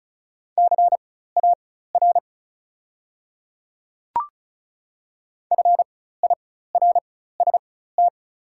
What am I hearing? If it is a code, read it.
Morse code, heard as a keyed tone, about 36 wpm, spaced as for 16 wpm, 700 Hz. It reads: CAR FIRST
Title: 4x Character Spacing - Top 300 Words - 35wpm